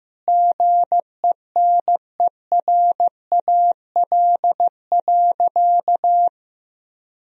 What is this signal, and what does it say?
Morse code, heard as a keyed tone, about 15 wpm, 700 Hz. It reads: GENERAL.
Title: Sentences from Top 1000 Words - Review Part 3 - 15wpm